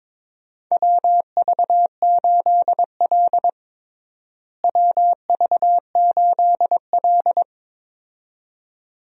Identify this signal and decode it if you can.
Morse code, heard as a keyed tone, about 22 wpm, 700 Hz. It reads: WV8L WV8L
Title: USA Call Signs - 4 Characters - Repeated 2x - 22wpm